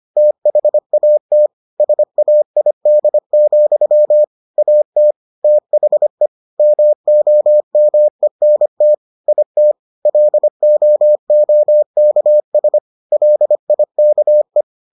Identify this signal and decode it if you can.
Morse code, heard as a keyed tone, about 25 wpm, 600 Hz. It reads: THAT SAID, AT THE MOMENT IT LOOKS LIKE